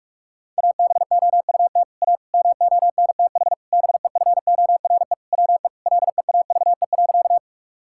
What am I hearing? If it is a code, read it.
Morse code, heard as a keyed tone, about 44 wpm, 690 Hz. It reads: ABOUT A MONTH BEFORE WE LEAVE.
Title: Sentences from Top 1000 Words - Review Part 5 - 45wpm